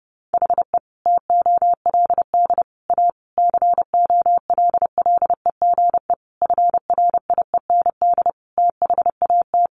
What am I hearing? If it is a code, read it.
Morse code, heard as a keyed tone, about 30 wpm, 700 Hz. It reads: HE TOLD A COLLEGE FRIEND THAT
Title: Sentences from Top 1300 Words 30wpm